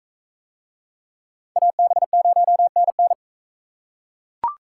Morse code, heard as a keyed tone, about 39 wpm, 700 Hz. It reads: AB0NN